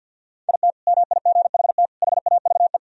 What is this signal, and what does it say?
Morse code, heard as a keyed tone, about 50 wpm, 700 Hz. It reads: IT MIGHT HAVE
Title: Sentences from Top 1000 Words - Review Part 3 - 50wpm